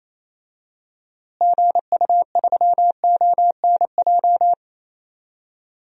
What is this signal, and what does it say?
Morse code, heard as a keyed tone, about 27 wpm, 700 Hz. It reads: GU3ONJ
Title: DXCC-100 Call Signs - 6 Characters - Repeated 2x - 28wpm